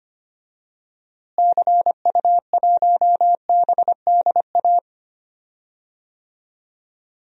Morse code, heard as a keyed tone, about 25 wpm, 700 Hz. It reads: CU1BDA
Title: DXCC-100 Call Signs - 6 Characters - Repeated 2x - 25wpm